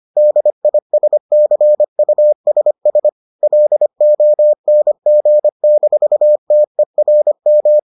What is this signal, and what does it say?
Morse code, heard as a keyed tone, about 25 wpm, 600 Hz. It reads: DISCUSS LONG-TERM